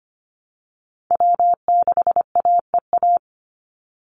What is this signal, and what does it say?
Morse code, heard as a keyed tone, about 25 wpm, 700 Hz. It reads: W6AEA
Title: Call Signs Repeated 2x - 25wpm